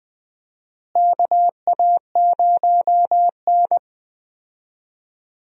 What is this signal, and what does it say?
Morse code, heard as a keyed tone, about 20 wpm, 700 Hz. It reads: KA0N
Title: USA Call Signs - 4 Characters - Repeated 3x - 20wpm